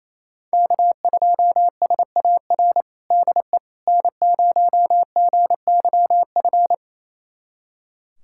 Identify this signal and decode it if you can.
Morse code, heard as a keyed tone, about 28 wpm, 700 Hz. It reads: K2SAR DE N0GYF
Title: USA Callsign DE Callsign - 5 Characters - 28wpm